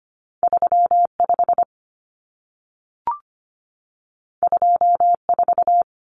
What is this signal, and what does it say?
Morse code, heard as a keyed tone, about 25 wpm, 700 Hz. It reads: 35 24
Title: Common Cut Numbers - 2-Digit 25wpm